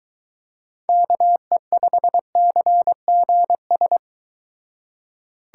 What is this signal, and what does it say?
Morse code, heard as a keyed tone, about 23 wpm, 700 Hz. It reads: KE5CGS